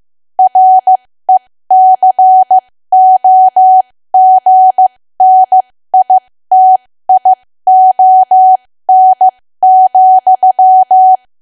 Morse code, heard as a keyed tone, about 15 wpm, spaced as for 13 wpm, 750 Hz. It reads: RECOGNITION,
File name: cwa / W1AW-Level2 / 13WPM